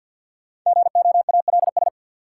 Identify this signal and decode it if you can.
Morse code, heard as a keyed tone, about 46 wpm, 700 Hz. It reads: GOALS
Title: Sentences from Top 1000 Words - Review Part 6 - New Word Drill - 50wpm